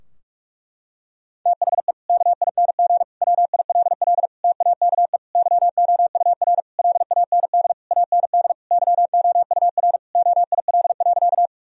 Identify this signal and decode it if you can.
Morse code, heard as a keyed tone, about 41 wpm, 700 Hz. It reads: THE KING WILL TAKE YOUR LAND AND YOUR OIL.
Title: Sentences from Top 1000 Words - Review Part 4 - 45wpm